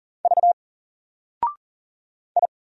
Morse code, heard as a keyed tone, about 40 wpm, 700 Hz. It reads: V I
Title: Single Letter-Number 40wpm